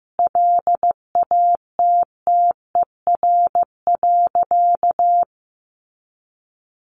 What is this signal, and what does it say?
Morse code, heard as a keyed tone, about 15 wpm, 700 Hz. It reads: LATTER.
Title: A Sound Of Thunder 15wpm